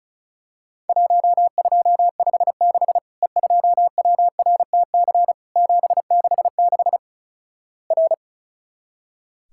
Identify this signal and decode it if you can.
Morse code, heard as a keyed tone, about 36 wpm, 700 Hz. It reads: 1256 E2WRTC 766 R
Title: Worked All Europe DX Contest - QTC Exchange - 35wpm